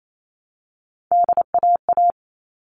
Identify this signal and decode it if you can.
Morse code, heard as a keyed tone, about 28 wpm, 700 Hz. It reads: DAA
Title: All Cut Numbers - 3-Digit 28wpm